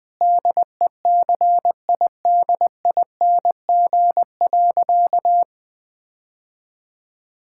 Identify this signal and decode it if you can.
Morse code, heard as a keyed tone, about 20 wpm, 700 Hz. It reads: DECIDING.
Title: Difficult Sentences with Hints - 20wpm